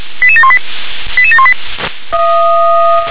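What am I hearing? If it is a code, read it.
Five-tone selective call (ZVEI), heard as quick runs of five tones, tones of about 70 ms. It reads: 88718 88718
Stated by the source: ZVEI five-tone call received by SDR.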